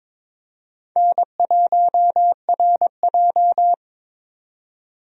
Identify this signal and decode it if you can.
Morse code, heard as a keyed tone, about 22 wpm, 700 Hz. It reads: N1RJ
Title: USA Call Signs - 4 Characters - 22wpm